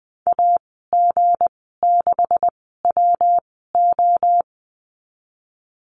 Morse code, heard as a keyed tone, about 20 wpm, spaced as for 14 wpm, 700 Hz. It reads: AG6WO
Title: 161 Lesson 26g - Letter g - Call Signs 20wpm